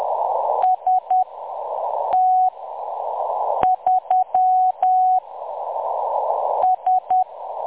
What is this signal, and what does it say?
Morse code, heard as a keyed tone, about 10 wpm, spaced as for 6 wpm, 740 Hz. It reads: ST3 S
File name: M23-ST3- -0858utc-10184khz